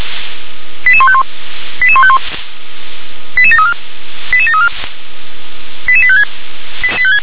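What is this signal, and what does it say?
Five-tone selective call (ZVEI), heard as quick runs of five tones, tones of about 70 ms. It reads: 88151 88151 88734 88734 88856
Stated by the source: ZVEI five-tone selective call received on an SDR.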